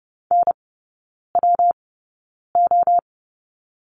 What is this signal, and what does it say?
Morse code, heard as a keyed tone, about 30 wpm, spaced as for 9 wpm, 700 Hz. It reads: NWO
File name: Lesson 19e - Letter w - Three Characters 30wpm